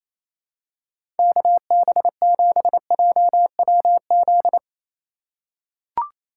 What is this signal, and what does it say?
Morse code, heard as a keyed tone, about 28 wpm, 700 Hz. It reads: KB7JWZ
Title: USA Call Signs - 6 Characters - 3x Character Spacing - 28wpm